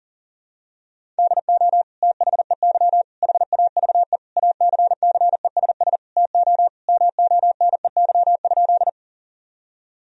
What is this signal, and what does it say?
Morse code, heard as a keyed tone, about 40 wpm, 690 Hz. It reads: DO THEY HAVE ACCESS TO MONEY?
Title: Sentences from Top 1000 Words - Review Part 4 - 40wpm